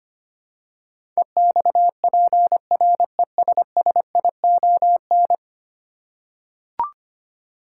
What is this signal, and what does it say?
Morse code, heard as a keyed tone, about 25 wpm, 700 Hz. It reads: EXPRESSION
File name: 10 Letter Words - First 3 Letters Given - 25wpm